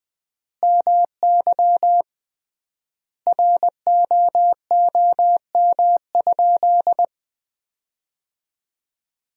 Morse code, heard as a keyed tone, about 20 wpm, 700 Hz. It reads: MY ROOM?
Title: 3x Word-Spacing - Sentences from Top 300 Words - 20wpm